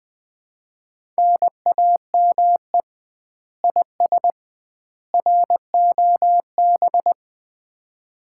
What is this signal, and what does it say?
Morse code, heard as a keyed tone, about 20 wpm, 700 Hz. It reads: NAME IS ROB
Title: Common QSO Phrases - 2x Word-Spacing - 20wpm